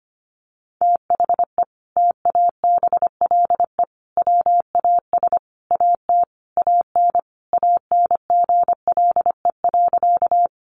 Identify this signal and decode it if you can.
Morse code, heard as a keyed tone, about 25 wpm, 700 Hz. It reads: THE TABLE WAS AT AN ANGLE.
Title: Sentences from Top 1500 Words 25wpm